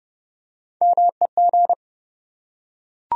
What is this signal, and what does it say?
Morse code, heard as a keyed tone, about 30 wpm, 700 Hz. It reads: MEG
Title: USA Common Names - Max 3 Letters - 30wpm